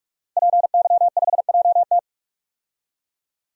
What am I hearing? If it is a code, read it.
Morse code, heard as a keyed tone, about 42 wpm, 700 Hz. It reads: PY5JT
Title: DXCC-100 Call Signs - 5 Characters - 45wpm